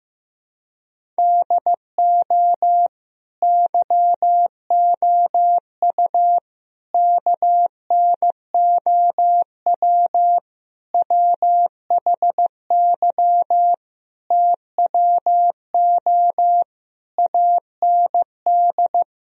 Morse code, heard as a keyed tone, about 16 wpm, 700 Hz. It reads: DO YOU KNOW WHY TWO AND
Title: Sentences from Top 1000 Words - Review Part 1 - 15wpm